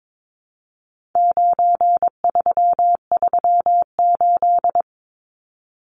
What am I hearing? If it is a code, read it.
Morse code, heard as a keyed tone, about 22 wpm, 700 Hz. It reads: 9338
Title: Numbers - 4-Digit 22wpm